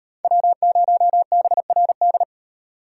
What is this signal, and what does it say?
Morse code, heard as a keyed tone, about 35 wpm, 700 Hz. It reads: W0BRD